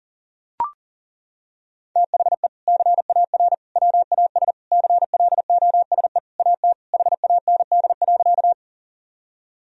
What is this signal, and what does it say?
Morse code, heard as a keyed tone, about 40 wpm, 700 Hz. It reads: THE CAR WAS CLOSE AT HAND.
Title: Sentences from Top 1000 Words - Review Part 1 - 40wpm